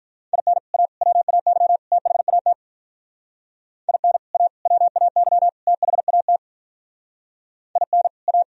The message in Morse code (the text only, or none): IN A WAY THAT IN A WAY THAT IN A